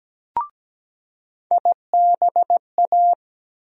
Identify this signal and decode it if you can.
Morse code, heard as a keyed tone, about 17 wpm, 700 Hz. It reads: IBA
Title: USA Call Sign Postfixes - 3 Letters 17wpm